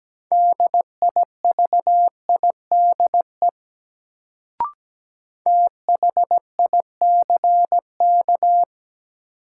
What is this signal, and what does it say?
Morse code, heard as a keyed tone, about 17 wpm, 700 Hz. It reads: DIVIDE THICK
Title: Sentences from Top 1600 Words - New Word Drill Alt - 17wpm